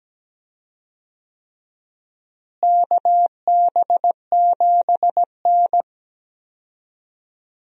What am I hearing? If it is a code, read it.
Morse code, heard as a keyed tone, about 17 wpm, 700 Hz. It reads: KB7N
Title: USA Call Signs - 4 Characters - Repeated 3x - 17wpm